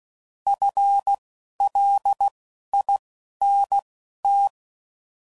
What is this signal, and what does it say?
Morse code, heard as a keyed tone, about 16 wpm, spaced as for 11 wpm, 800 Hz. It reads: FLINT